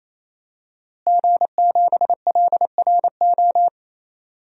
Morse code, heard as a keyed tone, about 27 wpm, 700 Hz. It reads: G7LRO
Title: DXCC-100 Call Signs - 5 Characters - Repeated 3x - 28wpm